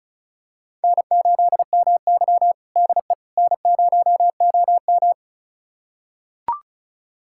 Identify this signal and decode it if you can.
Morse code, heard as a keyed tone, about 34 wpm, 700 Hz. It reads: N8MY DE N0OM
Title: USA Callsign DE Callsign - 4 Characters - 35wpm